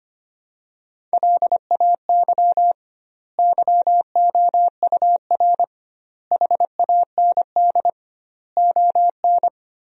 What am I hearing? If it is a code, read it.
Morse code, heard as a keyed tone, about 25 wpm, 700 Hz. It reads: LAY YOUR HAND ON
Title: 2x Word-Spacing - Sentences from Top 400 Words - 25wpm